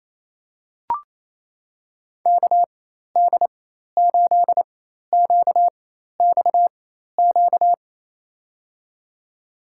Morse code, heard as a keyed tone, about 28 wpm, spaced as for 13 wpm, 700 Hz. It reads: KD8QXQ
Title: USA Call Signs - 4x Character Spacing - 28wpm